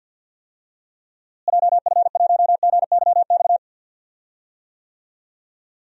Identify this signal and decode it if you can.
Morse code, heard as a keyed tone, about 47 wpm, 700 Hz. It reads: JF1GYX
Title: DXCC-100 Call Signs - 6 Characters - Repeated 3x - 50wpm